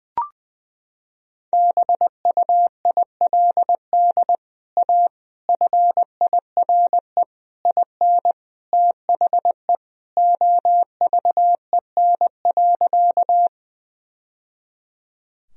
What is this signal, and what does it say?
Morse code, heard as a keyed tone, about 20 wpm, 700 Hz. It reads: BUILD A FIRE IN THE OVEN.
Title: Sentences from Top 1000 Words - Review Part 4 - 20wpm